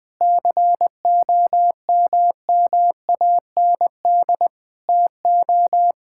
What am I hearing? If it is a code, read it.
Morse code, heard as a keyed tone, about 20 wpm, 700 Hz. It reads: COMMAND TO
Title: Sentences from Top 1600 Words 20wpm